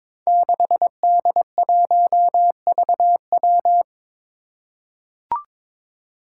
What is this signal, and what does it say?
Morse code, heard as a keyed tone, about 22 wpm, 700 Hz. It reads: 6D1VW
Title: Five Challenging Letter-Number Combinations 22wpm